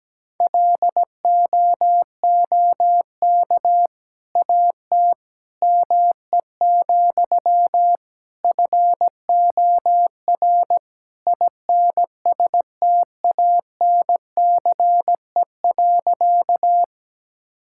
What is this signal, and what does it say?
Morse code, heard as a keyed tone, about 17 wpm, 700 Hz. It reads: LOOK AT ME, FOR INSTANCE.
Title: Sentences from Top 1000 Words - Review Part 5 - 17wpm